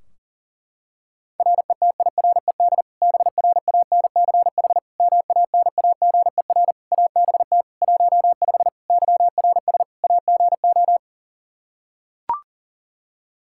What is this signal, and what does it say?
Morse code, heard as a keyed tone, about 40 wpm, 700 Hz. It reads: RETIRED BRANCH MANAGER ABT 15 YRS AGO
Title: Practice QSO - Exchange 3 - More Info - 40wpm